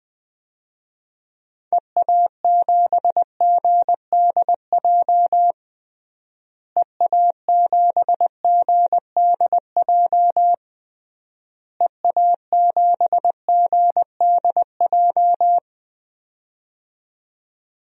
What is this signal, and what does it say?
Morse code, heard as a keyed tone, about 21 wpm, 700 Hz. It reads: EA7GDJ EA7GDJ EA7GDJ
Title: DXCC-100 Call Signs - 6 Characters - Repeated 3x - 20wpm